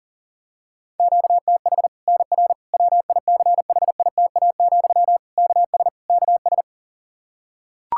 Morse code, heard as a keyed tone, about 37 wpm, 700 Hz. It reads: QTH NR WICHITA, KS KS
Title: Practice QSO - Exchange 1 - Beginning - 40wpm